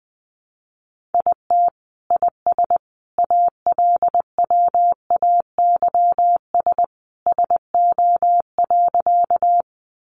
Morse code, heard as a keyed tone, about 20 wpm, 700 Hz. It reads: IT IS ALWAYS SO.
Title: Sentences from Top 300 Words 20wpm